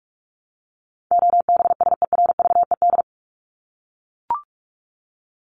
Morse code, heard as a keyed tone, about 44 wpm, 700 Hz. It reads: OBSERVED
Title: ED Suffix - Top 100 Words with Base Words 45wpm